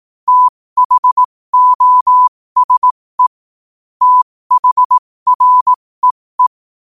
Morse code, heard as a keyed tone, about 18 wpm, spaced as for 15 wpm, 1000 Hz. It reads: THOSE THREE
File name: qotdfiles / 15wpm